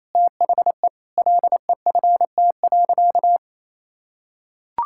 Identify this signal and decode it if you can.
Morse code, heard as a keyed tone, about 28 wpm, 700 Hz. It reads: THE LEFT.
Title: Sentences from Top 1600 Words 28wpm